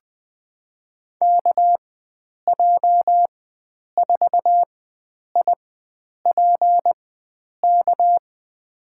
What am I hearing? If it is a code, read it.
Morse code, heard as a keyed tone, about 20 wpm, spaced as for 9 wpm, 700 Hz. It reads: KJ4IPK
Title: USA Call Signs - 4x Character Spacing - 20wpm